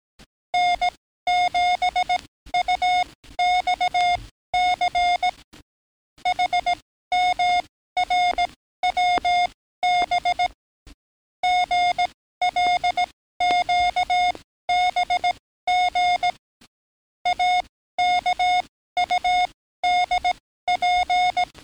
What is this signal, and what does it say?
Morse code, heard as a keyed tone, about 18 wpm, spaced as for 13 wpm, 720 Hz. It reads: N7UXC HMRWB GLQBG AKUDP
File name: training / NAVPERS 11049to11060 / NAVPERS 11056 RA RB